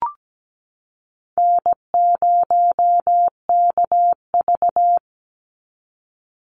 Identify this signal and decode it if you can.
Morse code, heard as a keyed tone, about 17 wpm, 700 Hz. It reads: N0KV
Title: Call Signs Repeated 3x - 17wpm